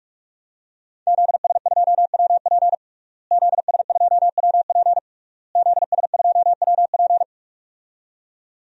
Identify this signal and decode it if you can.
Morse code, heard as a keyed tone, about 41 wpm, 700 Hz. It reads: ZS2WP ZS2WP ZS2WP
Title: DXCC-100 Call Signs - 5 Characters - Repeated 3x - 45wpm